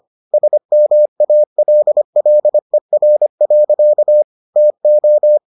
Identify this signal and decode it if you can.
Morse code, heard as a keyed tone, about 25 wpm, 600 Hz. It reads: SMALLER. TO